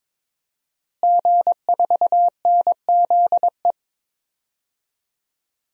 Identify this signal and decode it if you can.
Morse code, heard as a keyed tone, about 22 wpm, 700 Hz. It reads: G4NZE
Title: DXCC-100 Call Signs - 5 Characters - Repeated 2x - 22wpm